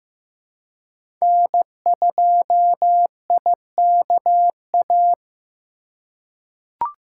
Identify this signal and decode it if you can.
Morse code, heard as a keyed tone, about 15 wpm, 700 Hz. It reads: N2IKA
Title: USA Call Signs - 3x Character Spacing - 15wpm